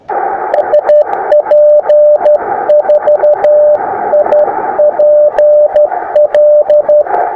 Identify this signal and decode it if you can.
Morse code, heard as a keyed tone, about 17 wpm, spaced as for 14 wpm, 600 Hz, with radio noise. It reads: SP4IPL
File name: attachments / sp4ipl 20m